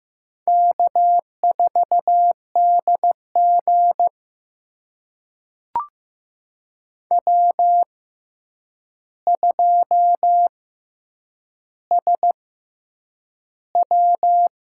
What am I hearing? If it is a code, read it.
Morse code, heard as a keyed tone, about 15 wpm, spaced as for 8 wpm, 700 Hz. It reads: K4DG W2SW
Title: USA Call Signs - 6x Character Spacing - 15wpm